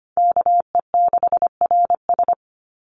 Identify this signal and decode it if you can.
Morse code, heard as a keyed tone, about 25 wpm, 700 Hz. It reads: KE6RS
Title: NA SOTA Chaser Callsigns 25wpm